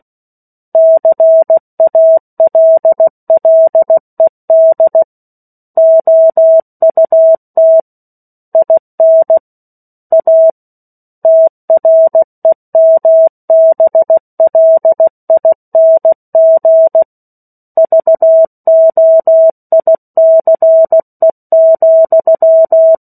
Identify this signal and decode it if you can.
Morse code, heard as a keyed tone, about 16 wpm, 650 Hz. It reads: CALLED OUT IN A TREMBLING VOICE,
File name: 03-Chapter-03 16 WPM